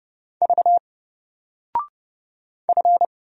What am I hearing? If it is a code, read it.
Morse code, heard as a keyed tone, about 30 wpm, 700 Hz. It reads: V F
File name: Single-Letter-Number-30wpm